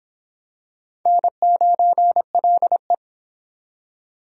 Morse code, heard as a keyed tone, about 22 wpm, 700 Hz. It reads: N9LE